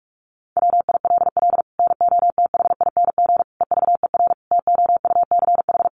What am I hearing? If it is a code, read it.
Morse code, heard as a keyed tone, about 44 wpm, 700 Hz. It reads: WILL NOTHING EVER TOUCH
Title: Sentences from Top 1200 Words 45wpm